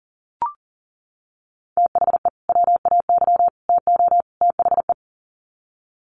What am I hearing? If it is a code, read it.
Morse code, heard as a keyed tone, about 40 wpm, 700 Hz. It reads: THE WAY TO THE
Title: Sets of 4 Words 40wpm